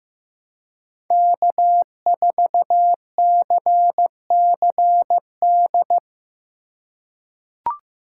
Morse code, heard as a keyed tone, about 15 wpm, 700 Hz. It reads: K4CCD
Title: USA Call Signs - 5 Characters - 3x Character Spacing - 15wpm